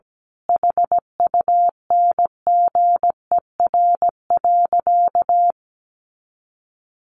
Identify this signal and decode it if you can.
Morse code, heard as a keyed tone, about 17 wpm, 700 Hz. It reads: HUNGER.
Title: A Sound Of Thunder 17wpm